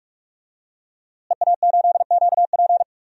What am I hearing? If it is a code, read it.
Morse code, heard as a keyed tone, about 46 wpm, 700 Hz. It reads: EA8QP
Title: DXCC-100 Call Signs - 5 Characters - Repeated 3x - 45wpm